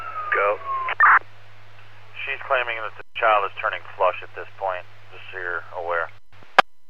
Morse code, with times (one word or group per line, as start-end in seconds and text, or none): none